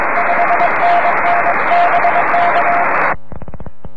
Morse code, heard as a keyed tone, about 22 wpm, 700 Hz, with noise. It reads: 5NN BK
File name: om3pv 70mhz / zaznam / hv0a